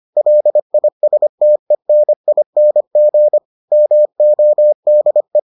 Morse code, heard as a keyed tone, about 25 wpm, 600 Hz. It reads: LISTENING MODE